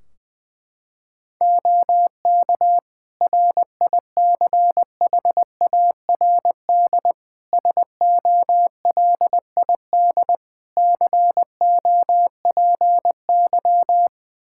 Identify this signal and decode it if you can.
Morse code, heard as a keyed tone, about 20 wpm, 700 Hz. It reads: OK RICHARD SOLID COPY
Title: Practice QSO - Exchange 3 - More Info - 20wpm